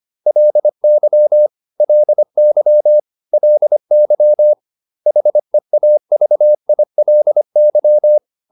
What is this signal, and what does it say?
Morse code, heard as a keyed tone, about 25 wpm, 600 Hz. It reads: LY LY LY HEAVILY